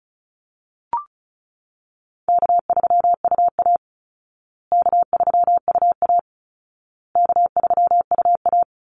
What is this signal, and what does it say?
Morse code, heard as a keyed tone, about 34 wpm, 700 Hz. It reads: K3UA K3UA K3UA
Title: Easy Call Signs Repeated 3x - 35wpm